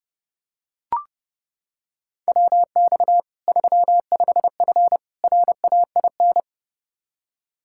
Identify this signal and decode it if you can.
Morse code, heard as a keyed tone, about 29 wpm, 700 Hz. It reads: WX 35F RAIN
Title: K1USN SST QSO Practice 30wpm